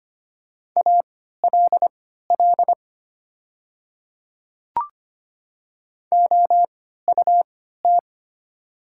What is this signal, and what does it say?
Morse code, heard as a keyed tone, about 25 wpm, spaced as for 14 wpm, 700 Hz. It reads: ALL OUT
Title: 3x Character Spacing - Top 100 Words - 25wpm